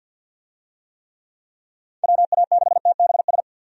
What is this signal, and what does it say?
Morse code, heard as a keyed tone, about 50 wpm, 700 Hz. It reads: WA6TBS
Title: USA Call Signs - 6 Characters - 50wpm